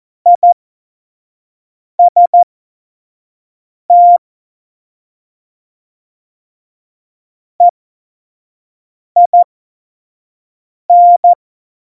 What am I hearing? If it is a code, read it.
Morse code, heard as a keyed tone, about 14 wpm, spaced as for 5 wpm, 700 Hz. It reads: IST EIN
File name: Morsekurs / Witze / Witz 09. April 2019, 25 F70